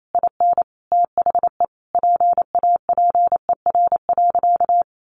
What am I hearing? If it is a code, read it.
Morse code, heard as a keyed tone, about 28 wpm, 700 Hz. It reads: IN THE PAPER.
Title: Sentences from Top 300 Words 28wpm